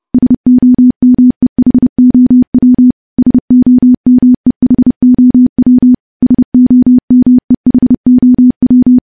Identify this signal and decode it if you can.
Morse code, heard as a keyed tone, about 30 wpm, 250 Hz. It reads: SOMEHOW SOMEHOW SOMEHOW
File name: morse / morse-training-30WPM